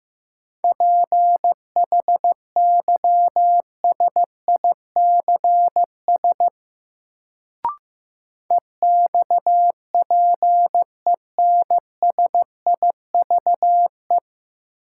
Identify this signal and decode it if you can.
Morse code, heard as a keyed tone, about 15 wpm, 700 Hz. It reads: PHYSICS EXPENSIVE
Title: Sentences from Top 1500 Words - New Word Drill Alt - 15wpm